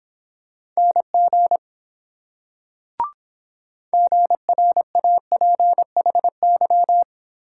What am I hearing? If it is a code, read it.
Morse code, heard as a keyed tone, about 22 wpm, 700 Hz. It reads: NG GRAPHY